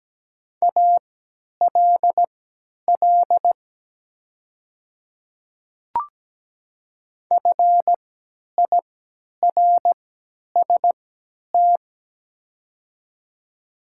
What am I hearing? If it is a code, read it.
Morse code, heard as a keyed tone, about 17 wpm, spaced as for 10 wpm, 700 Hz. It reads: ALL FIRST
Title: 3x Character Spacing - Top 100 Words - 17wpm